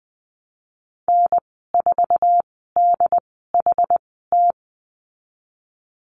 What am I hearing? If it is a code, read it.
Morse code, heard as a keyed tone, about 20 wpm, spaced as for 15 wpm, 700 Hz. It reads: N4DHT
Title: 082 Lesson 14g - Number 2 - Call Signs 20wpm